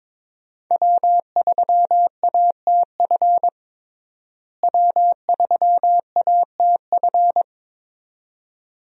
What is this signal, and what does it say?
Morse code, heard as a keyed tone, about 21 wpm, 700 Hz. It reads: W3ATF W3ATF
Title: USA Call Signs - 5 Characters - Repeated 2x - 22wpm